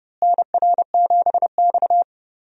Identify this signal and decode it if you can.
Morse code, heard as a keyed tone, about 30 wpm, 700 Hz. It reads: NR7X
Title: USA Call Signs - 4 Characters - Repeated 3x - 30wpm